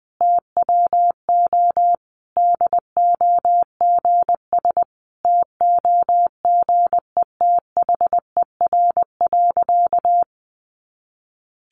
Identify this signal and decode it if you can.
Morse code, heard as a keyed tone, about 19 wpm, 700 Hz. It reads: TWO DOGS TOGETHER.
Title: Easy Phrases 20wpm